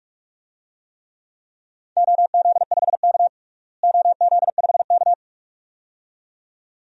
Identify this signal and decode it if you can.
Morse code, heard as a keyed tone, about 46 wpm, 700 Hz. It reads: OZ5K OZ5K
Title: DXCC-100 Call Signs - 4 Characters - Repeated 2x - 45wpm